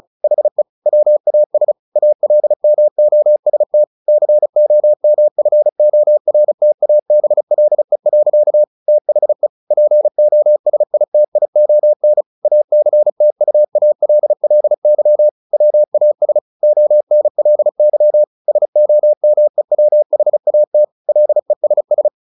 Morse code, heard as a keyed tone, about 34 wpm, 600 Hz. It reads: HE WAS ALMOST COMFORTABLE. THE POSITION ACTUALLY WAS ONLY SOMEWHAT LESS
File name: the old man and the sea / the old man and the sea 1 35wpm